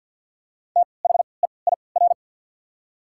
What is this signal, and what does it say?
Morse code, heard as a keyed tone, about 53 wpm, spaced as for 29 wpm, 700 Hz. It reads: THEIR
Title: 3x Character Spacing - Top 100 Words - 50wpm